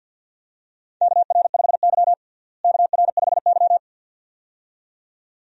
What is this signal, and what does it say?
Morse code, heard as a keyed tone, about 52 wpm, 700 Hz. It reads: KR5Y KR5Y
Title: USA Call Signs - 4 Characters - Repeated 2x - 50wpm